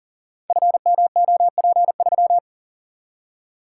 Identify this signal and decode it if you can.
Morse code, heard as a keyed tone, about 41 wpm, 700 Hz. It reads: FMOP3
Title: Code Groups - 5 Letter:Numbers - 40wpm